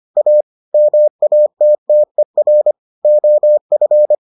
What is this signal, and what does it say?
Morse code, heard as a keyed tone, about 25 wpm, 600 Hz. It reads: A MATTER OF